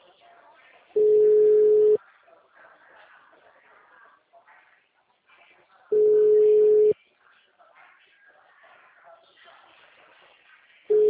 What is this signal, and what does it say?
Ringback tone (Europe, 425 Hz), with speech.